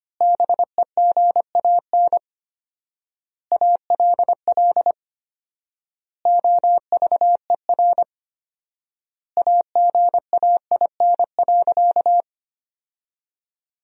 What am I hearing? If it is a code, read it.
Morse code, heard as a keyed tone, about 25 wpm, 700 Hz. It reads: BEGAN ALL OVER AGAIN.
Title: 4x Word-Spacing - Sentences from Top 500 Words - 25wpm